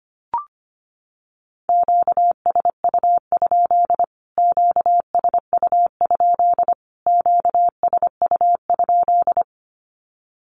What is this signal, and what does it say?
Morse code, heard as a keyed tone, about 25 wpm, 700 Hz. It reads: QSU? QSU? QSU?
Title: Q-Codes Repeated 3x 25wpm